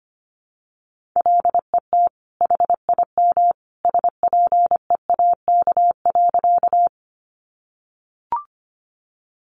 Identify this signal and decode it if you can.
Morse code, heard as a keyed tone, about 25 wpm, 700 Hz. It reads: LET HIM SPEAK.
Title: Sentences from Top 1100 Words 25wpm